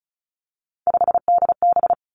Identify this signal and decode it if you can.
Morse code, heard as a keyed tone, about 36 wpm, 700 Hz. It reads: 5DB
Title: Three Challenging Letter-Number Combinations 35wpm